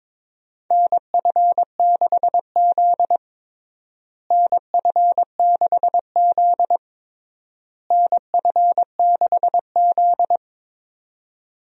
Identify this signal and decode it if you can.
Morse code, heard as a keyed tone, about 21 wpm, 700 Hz. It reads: NF6Z NF6Z NF6Z
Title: USA Call Signs - 4 Characters - Repeated 3x - 22wpm